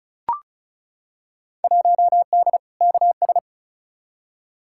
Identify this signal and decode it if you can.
Morse code, heard as a keyed tone, about 36 wpm, 700 Hz. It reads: 1D KS
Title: Field Day Exchanges - 35wpm